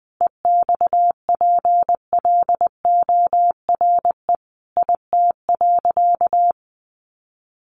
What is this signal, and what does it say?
Morse code, heard as a keyed tone, about 19 wpm, 700 Hz. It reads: EXPLORE IT.
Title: Sentences from Top 1100 Words 20wpm